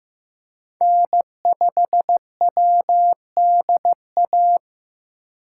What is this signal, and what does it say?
Morse code, heard as a keyed tone, about 15 wpm, 700 Hz. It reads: N5WDA
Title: USA Call Signs - 5 Characters - 8x Character Spacing - 15wpm